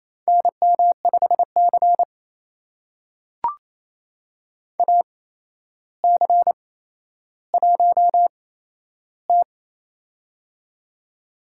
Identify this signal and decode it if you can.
Morse code, heard as a keyed tone, about 28 wpm, spaced as for 12 wpm, 700 Hz. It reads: NM5C AC1T
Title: USA Call Signs - 4 Characters - 8x Character Spacing - 28wpm